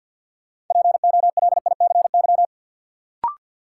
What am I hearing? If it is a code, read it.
Morse code, heard as a keyed tone, about 53 wpm, 700 Hz. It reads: POLICY
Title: Sentences from Top 1200 Words - New Word Drill Alt - 50wpm